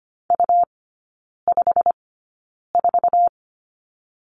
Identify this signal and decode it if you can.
Morse code, heard as a keyed tone, about 25 wpm, spaced as for 9 wpm, 700 Hz. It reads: U54
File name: Lesson 17e - Letter u - Three Characters 25wpm